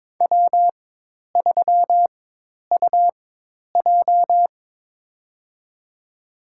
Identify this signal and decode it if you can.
Morse code, heard as a keyed tone, about 22 wpm, spaced as for 10 wpm, 700 Hz. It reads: W3UJ
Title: USA Call Signs - 4 Characters - 4x Character Spacing - 22wpm